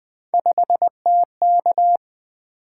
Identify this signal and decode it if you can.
Morse code, heard as a keyed tone, about 20 wpm, 700 Hz. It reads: 5TK
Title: Three Letter-Number Combinations - 20wpm